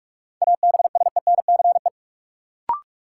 Morse code, heard as a keyed tone, about 46 wpm, 700 Hz. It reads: ABSENCE
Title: Sets of 4 Words - New Word Drill - 45wpm